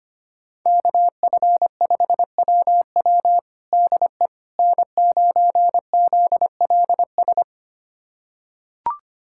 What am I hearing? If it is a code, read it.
Morse code, heard as a keyed tone, about 25 wpm, 700 Hz. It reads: KF5WW DE N9ZLS
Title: USA Callsign DE Callsign - 5 Characters - 25wpm